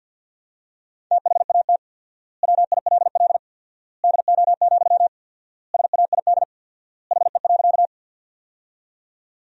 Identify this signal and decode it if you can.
Morse code, heard as a keyed tone, about 50 wpm, 700 Hz. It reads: THAT WILL DO, SAID HE.
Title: 4x Word-Spacing - Sentences from Top 100 Words - 50wpm